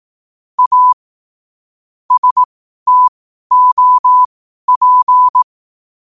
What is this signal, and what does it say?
Morse code, heard as a keyed tone, about 18 wpm, spaced as for 12 wpm, 1000 Hz. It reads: A STOP
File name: qotdfiles / 12wpm